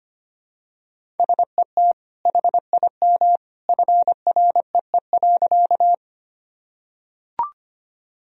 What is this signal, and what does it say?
Morse code, heard as a keyed tone, about 25 wpm, 700 Hz. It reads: SET HIM FREE.
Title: Sentences from Top 1000 Words - Review Part 1 - 25wpm